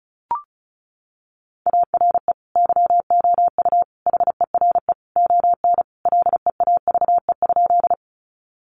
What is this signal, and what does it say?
Morse code, heard as a keyed tone, about 34 wpm, 700 Hz. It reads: ARE YOU HERE ON LEAVE?
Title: Sentences from Top 400 Words 35wpm